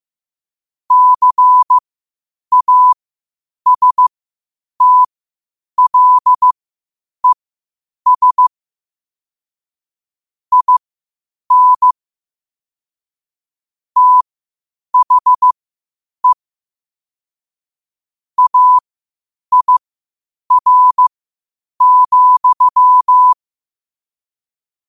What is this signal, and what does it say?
Morse code, heard as a keyed tone, about 15 wpm, spaced as for 8 wpm, 1000 Hz. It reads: CASTLES IN THE AIR,
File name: qotdfiles / 8wpm